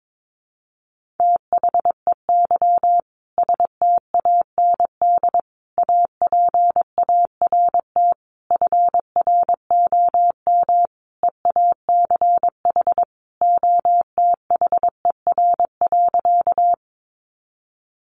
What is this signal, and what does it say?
Morse code, heard as a keyed tone, about 22 wpm, 700 Hz. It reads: THEY STAND APART FROM EACH OTHER.
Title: Easy Phrases 22wpm